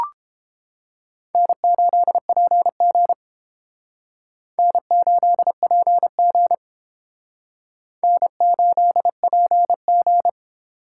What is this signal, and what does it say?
Morse code, heard as a keyed tone, about 32 wpm, 700 Hz. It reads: N8PG N8PG N8PG